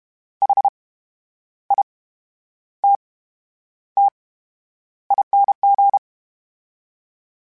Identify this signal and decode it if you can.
Morse code, heard as a keyed tone, about 32 wpm, 800 Hz. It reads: H I T T ING
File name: Special INGs / Special INGs-mp3s / VerySlowToFaster Ing @30